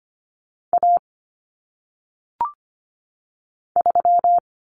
Morse code, heard as a keyed tone, about 25 wpm, 700 Hz. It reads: A 3
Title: Common Cut Numbers - 1-Digit 25wpm